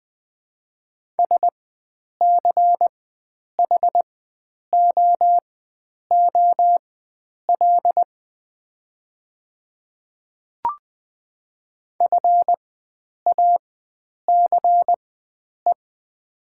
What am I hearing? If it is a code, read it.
Morse code, heard as a keyed tone, about 20 wpm, spaced as for 9 wpm, 700 Hz. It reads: SCHOOL FACE
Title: 4x Character Spacing - Top 500 Words - 20wpm